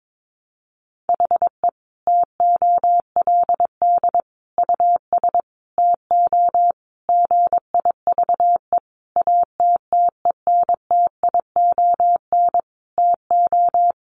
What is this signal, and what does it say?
Morse code, heard as a keyed tone, about 22 wpm, 700 Hz. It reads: HE TOLD US TO GIVE ATTENTION TO
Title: Sentences from Top 1000 Words 22wpm